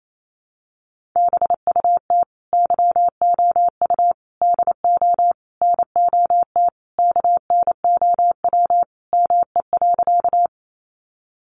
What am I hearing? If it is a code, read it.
Morse code, heard as a keyed tone, about 28 wpm, 700 Hz. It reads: BUT YOU DO NOT KNOW ME.
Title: Sentences from Top 600 Words 28wpm